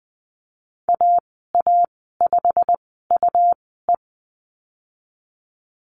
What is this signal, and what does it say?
Morse code, heard as a keyed tone, about 20 wpm, spaced as for 15 wpm, 700 Hz. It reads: AA5UE
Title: 147 Lesson 24g - Letter y - Call Signs 20wpm